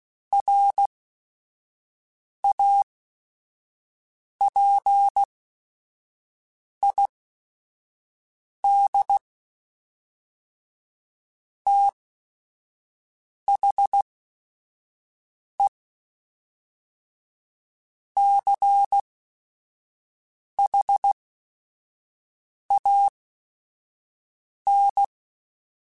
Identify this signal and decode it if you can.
Morse code, heard as a keyed tone, about 16 wpm, spaced as for 5 wpm, 800 Hz. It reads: RAPID THE CHAN
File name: Technical Subjects / Technical Subjects-mp3s / XC @5